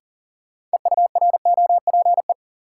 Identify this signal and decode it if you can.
Morse code, heard as a keyed tone, about 40 wpm, 700 Hz. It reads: EUROPE
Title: Sets of 5 Words Encore - New Word Drill - 40wpm